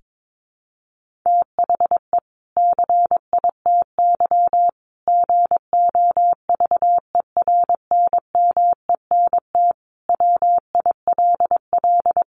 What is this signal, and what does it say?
Morse code, heard as a keyed tone, about 22 wpm, 700 Hz. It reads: THE CITY GOVERNMENT WILL
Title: Sentences from Top 1500 Words 22wpm